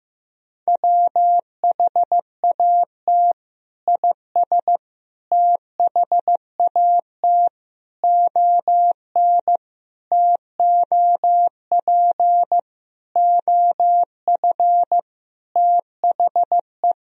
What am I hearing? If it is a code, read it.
Morse code, heard as a keyed tone, about 15 wpm, 700 Hz. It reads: WHAT IS THAT ON TOP OF THE
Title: Sentences from Top 1000 Words - Review Part 2 - 15wpm